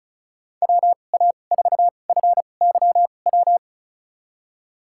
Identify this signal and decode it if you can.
Morse code, heard as a keyed tone, about 36 wpm, spaced as for 25 wpm, 700 Hz. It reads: WA4FYW
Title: USA Call Signs - 6 Characters - 2x Character Spacing - 35wpm